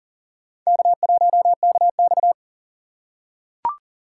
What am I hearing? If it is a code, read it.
Morse code, heard as a keyed tone, about 40 wpm, 700 Hz. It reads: K1KX
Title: USA Call Signs - 4 Characters - 6x Character Spacing - 40wpm